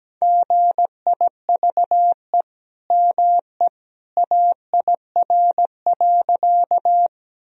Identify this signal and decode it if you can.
Morse code, heard as a keyed tone, about 17 wpm, 700 Hz. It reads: GIVE ME AIR.
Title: Sentences from Top 1000 Words - Review Part 1 - 17wpm